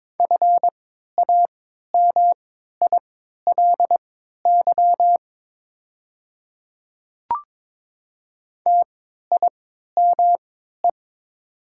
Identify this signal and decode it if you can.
Morse code, heard as a keyed tone, about 21 wpm, spaced as for 12 wpm, 700 Hz. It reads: FAMILY TIME
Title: 3x Character Spacing - Top 500 Words - 22wpm